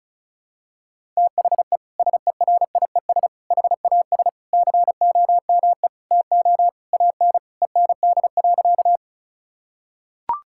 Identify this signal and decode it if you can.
Morse code, heard as a keyed tone, about 34 wpm, 700 Hz. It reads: THE SERIES HAS COME TO AN END.
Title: Sentences from Top 1000 Words - Review Part 5 - 35wpm